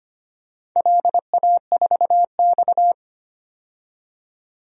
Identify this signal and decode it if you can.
Morse code, heard as a keyed tone, about 25 wpm, 700 Hz. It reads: LA4X
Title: DXCC-100 Call Signs - 4 Characters - 25wpm